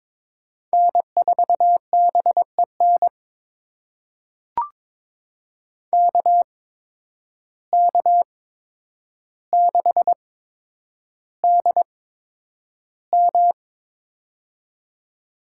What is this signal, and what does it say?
Morse code, heard as a keyed tone, about 22 wpm, spaced as for 9 wpm, 700 Hz. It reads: N4BEN KK6DM
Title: USA Call Signs - 5 Characters - 8x Character Spacing - 22wpm